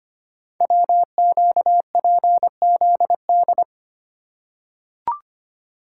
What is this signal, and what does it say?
Morse code, heard as a keyed tone, about 25 wpm, 700 Hz. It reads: WQPZD
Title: Code Groups - 5 Letter:Numbers - 25wpm